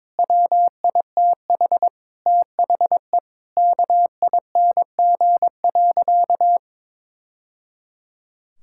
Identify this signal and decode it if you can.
Morse code, heard as a keyed tone, about 22 wpm, 700 Hz. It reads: WITH THE KING.
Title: Sentences from Top 1000 Words - Review Part 6 - 22wpm